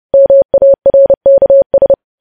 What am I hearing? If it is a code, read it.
Morse code, heard as a keyed tone, about 30 wpm, 550 Hz. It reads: MARKS
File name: podcast / News headlines-1x-30x30